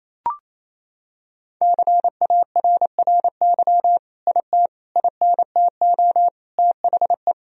Morse code, heard as a keyed tone, about 28 wpm, 700 Hz. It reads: CARRY IT INTO THE
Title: Sentences from Top 1000 Words - Review Part 1 - 28wpm